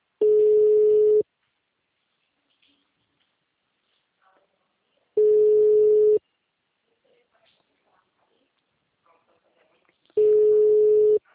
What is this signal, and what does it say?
Ringback tone (Europe, 425 Hz).
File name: out- -2014- - - .15592